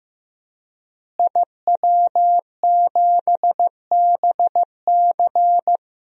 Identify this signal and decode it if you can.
Morse code, heard as a keyed tone, about 15 wpm, 700 Hz. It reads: IW7BC
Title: DXCC-100 Call Signs - 5 Characters - Repeated 3x - 15wpm